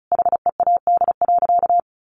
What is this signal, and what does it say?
Morse code, heard as a keyed tone, about 34 wpm, 700 Hz. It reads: HEAD.
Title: Sentences from Top 300 Words 35wpm